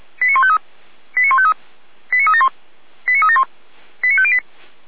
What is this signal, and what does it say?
Five-tone selective call (ZVEI), heard as quick runs of five tones, tones of about 80 ms. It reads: 89153 89153 89371 89371 89508
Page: /455/SDR/ZVEI/